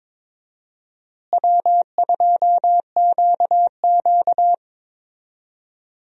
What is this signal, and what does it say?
Morse code, heard as a keyed tone, about 22 wpm, 700 Hz. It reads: W2QQ
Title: USA Call Signs - 4 Characters - Repeated 3x - 22wpm